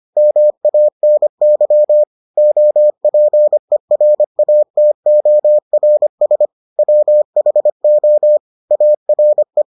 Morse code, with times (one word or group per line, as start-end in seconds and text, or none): 0.2-2.0 MANY
2.4-6.5 OPERATORS
6.8-8.4 WHO
8.7-9.6 ARE